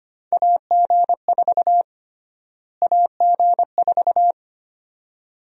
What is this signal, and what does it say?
Morse code, heard as a keyed tone, about 25 wpm, 700 Hz. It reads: AG4 AG4
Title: USA Call Sign Prefixes - Repeated 2x - 25wpm